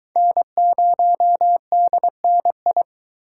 Morse code, heard as a keyed tone, about 24 wpm, 700 Hz. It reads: N0DNI